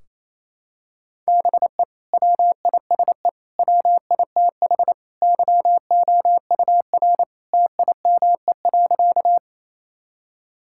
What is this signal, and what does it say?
Morse code, heard as a keyed tone, about 29 wpm, 700 Hz. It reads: BE WISE WITH YOUR TIME.
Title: Sentences from Top 1600 Words 28wpm